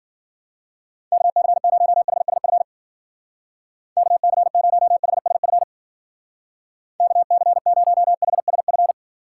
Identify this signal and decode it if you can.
Morse code, heard as a keyed tone, about 51 wpm, 700 Hz. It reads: KC0HSF KC0HSF KC0HSF